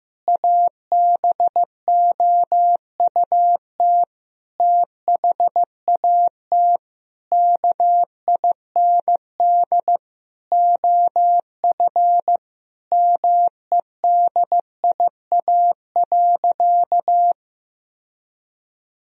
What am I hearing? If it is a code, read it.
Morse code, heard as a keyed tone, about 16 wpm, 700 Hz. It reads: ABOUT THAT KIND OF MEDIA.
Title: Sentences from Top 1000 Words - Review Part 4 - 15wpm